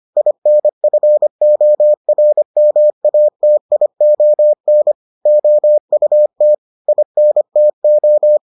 Morse code, heard as a keyed tone, about 25 wpm, 600 Hz. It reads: INFORMATION OUT INTO